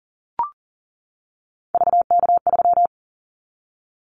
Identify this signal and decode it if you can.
Morse code, heard as a keyed tone, about 40 wpm, 700 Hz. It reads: VK3
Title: Callsign Prefixes for Top 100 DXCC - 40wpm